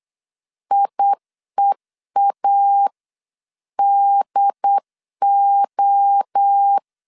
Morse code, heard as a keyed tone, about 8 wpm, 800 Hz. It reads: IEADO